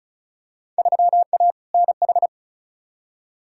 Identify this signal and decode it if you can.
Morse code, heard as a keyed tone, about 36 wpm, 700 Hz. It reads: 3A NH
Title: Field Day Exchanges - 35wpm